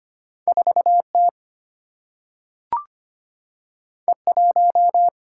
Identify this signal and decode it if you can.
Morse code, heard as a keyed tone, about 25 wpm, 690 Hz. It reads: 4T E1
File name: Two Letter-Number Combinations - 25wpm